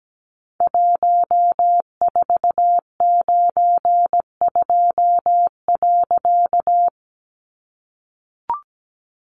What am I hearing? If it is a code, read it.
Morse code, heard as a keyed tone, about 17 wpm, 700 Hz. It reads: 1492.
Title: A Sound Of Thunder 17wpm